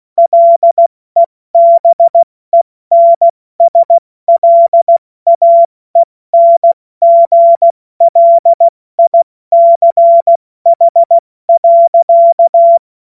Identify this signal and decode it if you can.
Morse code, heard as a keyed tone, about 16 wpm, 680 Hz. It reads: LEBENSLAENGLICH.
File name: Morsekurs / Witze / Witz 01. März 2023 70 F80